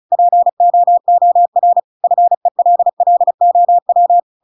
Morse code, heard as a keyed tone, about 34 wpm, 700 Hz. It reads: POOR FELLOW
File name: Pinocchio / Pinocchio ch8 35 35 ik1hge